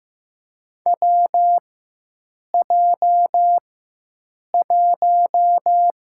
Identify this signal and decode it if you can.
Morse code, heard as a keyed tone, about 15 wpm, 700 Hz. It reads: W J 1
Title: USA Call Signs - 4 Characters - 4x Character Spacing - 15wpm